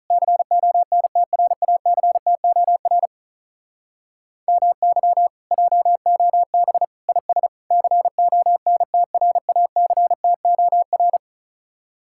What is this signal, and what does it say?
Morse code, heard as a keyed tone, about 46 wpm, 700 Hz. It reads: CONTRACTOR MY JOB IS CONTRACTOR